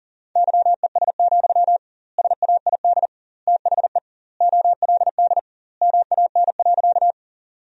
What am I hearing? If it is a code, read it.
Morse code, heard as a keyed tone, about 40 wpm, 700 Hz. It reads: YES, SAID THE OLD MAN.
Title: 2x Word-Spacing - Sentences from Top 500 Words - 40wpm